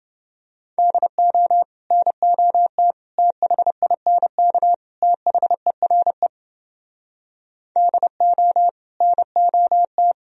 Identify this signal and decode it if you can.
Morse code, heard as a keyed tone, about 29 wpm, 700 Hz. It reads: DO NOT THINK THERE DO NOT